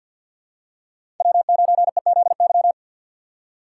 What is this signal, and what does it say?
Morse code, heard as a keyed tone, about 53 wpm, 700 Hz. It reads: W9EZY
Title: USA Call Signs - 5 Characters - 6x Character Spacing - 50wpm